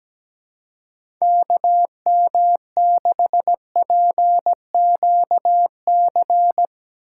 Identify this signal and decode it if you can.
Morse code, heard as a keyed tone, about 17 wpm, 700 Hz. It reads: KM6PQC
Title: USA Call Signs - 6 Characters - 4x Character Spacing - 17wpm